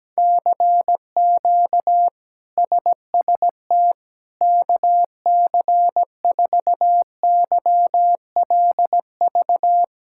Morse code, heard as a keyed tone, about 17 wpm, 700 Hz. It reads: CQ SST KC4YLV
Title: K1USN SST QSO Practice - 17wpm